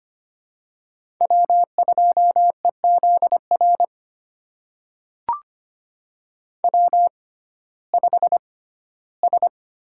Morse code, heard as a keyed tone, about 25 wpm, spaced as for 15 wpm, 700 Hz. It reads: W2EZR W5S
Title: USA Call Signs - 5 Characters - 6x Character Spacing - 25wpm